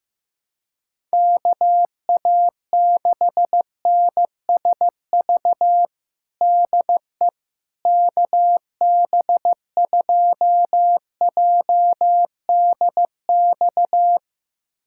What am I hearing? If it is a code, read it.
Morse code, heard as a keyed tone, about 16 wpm, 700 Hz. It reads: KA6NSV DE KB2JDX
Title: USA Callsign DE Callsign 15wpm